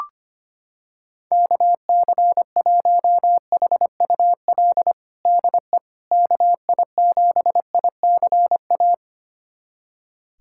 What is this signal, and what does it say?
Morse code, heard as a keyed tone, about 25 wpm, 700 Hz. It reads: KC1HUL DE KI7ICA
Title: USA Callsign DE Callsign - 6 Characters - 25wpm